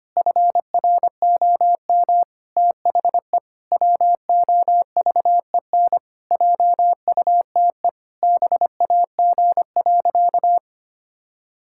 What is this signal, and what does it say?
Morse code, heard as a keyed tone, about 25 wpm, 700 Hz. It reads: FROM THE WOVEN JUTE BAG.
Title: Pangram Challenge v2 - 25wpm